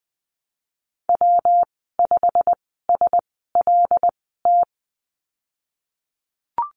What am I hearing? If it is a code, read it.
Morse code, heard as a keyed tone, about 20 wpm, spaced as for 15 wpm, 700 Hz. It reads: W5SLT
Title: 154 Lesson 25g - Letter p - Call Signs 20wpm